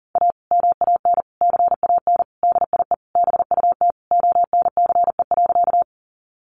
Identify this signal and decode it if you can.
Morse code, heard as a keyed tone, about 40 wpm, 700 Hz. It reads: A MAN CAN DIE BUT ONCE.
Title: Sentences from Top 1300 Words 40wpm